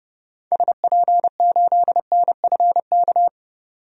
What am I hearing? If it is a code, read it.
Morse code, heard as a keyed tone, about 30 wpm, 700 Hz. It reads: SP8NFK
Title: DXCC-100 Call Signs - 6 Characters - 30wpm